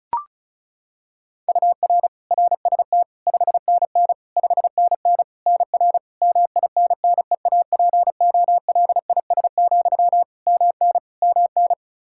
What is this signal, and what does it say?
Morse code, heard as a keyed tone, about 36 wpm, 700 Hz. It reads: UR RST 5NN 5NN NR MINNEAPOLIS, MN MN
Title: Practice QSO - Exchange 1 - Beginning - 35wpm